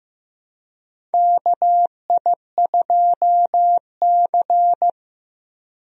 Morse code, heard as a keyed tone, about 15 wpm, 700 Hz. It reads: KI2C
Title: USA Call Signs - 4 Characters - Repeated 2x - 15wpm